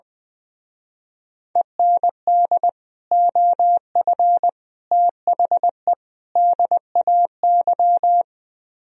Morse code, heard as a keyed tone, about 20 wpm, 700 Hz. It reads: END OF THE DAY
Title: Speed-Racing - Sets of 4 Words 20wpm